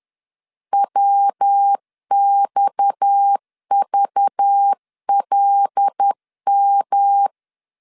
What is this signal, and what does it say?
Morse code, heard as a keyed tone, about 10 wpm, 800 Hz. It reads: WXVLM
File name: LEZIONI / 6 PKX0 / 6 PKX 10wpm 2426